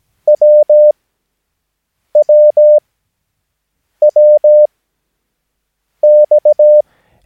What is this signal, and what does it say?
Morse code, heard as a keyed tone, about 17 wpm, spaced as for 6 wpm, 600 Hz. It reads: WWWX
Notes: Övningsmaterial med alfabetet A till Ö samt siffror 0 till 9 enligt International/Continental Morse Code.